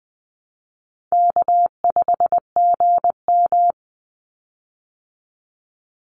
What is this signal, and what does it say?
Morse code, heard as a keyed tone, about 20 wpm, 700 Hz. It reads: K5GM
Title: Easy Call Signs Repeated 2x - 20wpm